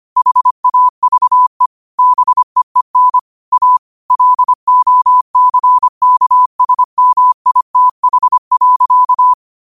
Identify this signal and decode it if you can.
Morse code, heard as a keyed tone, about 25 wpm, 1000 Hz. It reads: HAVE BEEN A LOCKSMITH.
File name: qotdfiles / 25wpm